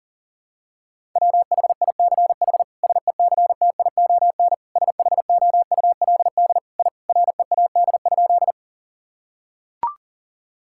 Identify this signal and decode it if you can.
Morse code, heard as a keyed tone, about 40 wpm, 700 Hz. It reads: WHICH SECTION SHOULD I READ?
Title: Sentences from Top 1000 Words - Review Part 5 - 40wpm